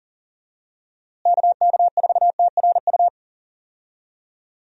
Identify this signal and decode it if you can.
Morse code, heard as a keyed tone, about 37 wpm, 700 Hz. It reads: KK4TRU
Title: USA Call Signs - 6 Characters - Repeated 3x - 40wpm